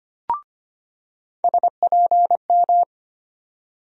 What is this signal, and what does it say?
Morse code, heard as a keyed tone, about 25 wpm, 700 Hz. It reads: SPM
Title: USA Call Sign Postfixes - 3 Letters 25wpm